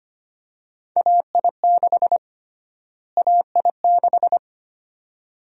Morse code, heard as a keyed tone, about 25 wpm, 700 Hz. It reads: AI6 AI6
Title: USA Call Sign Prefixes - Repeated 2x - 25wpm